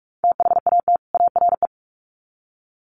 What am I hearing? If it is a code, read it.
Morse code, heard as a keyed tone, about 44 wpm, 700 Hz. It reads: THAT ARE
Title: Sets of 2 Words 45wpm